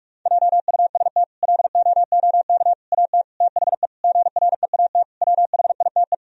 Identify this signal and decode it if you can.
Morse code, heard as a keyed tone, about 46 wpm, 700 Hz. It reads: JUST LOOK AT THE GREAT WHITE
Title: Sentences from Top 1000 Words - Review Part 1 - 45wpm